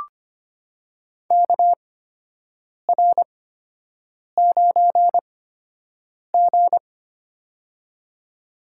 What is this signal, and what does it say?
Morse code, heard as a keyed tone, about 25 wpm, spaced as for 7 wpm, 700 Hz. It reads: KR9G
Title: USA Call Signs - 4 Characters - 8x Character Spacing - 25wpm